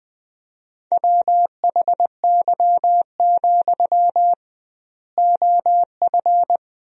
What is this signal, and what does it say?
Morse code, heard as a keyed tone, about 20 wpm, 700 Hz. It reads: WHY, OF
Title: 2x Word-Spacing - Sentences from Top 500 Words - 20wpm